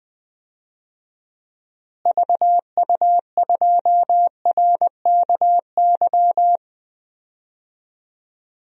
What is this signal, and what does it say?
Morse code, heard as a keyed tone, about 20 wpm, 700 Hz. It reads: VU2RKY
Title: DXCC-100 Call Signs - 6 Characters - Repeated 3x - 20wpm